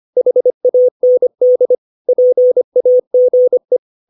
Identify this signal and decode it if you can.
Morse code, heard as a keyed tone, about 25 wpm, 500 Hz. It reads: HAND PAGE
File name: 632 words / 632 Words 25WPM